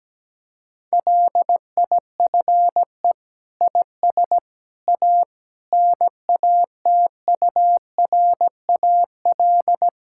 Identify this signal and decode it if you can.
Morse code, heard as a keyed tone, about 17 wpm, 700 Hz. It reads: LIFE IS A NATURAL
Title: Sentences from Top 1000 Words - Review Part 6 - 17wpm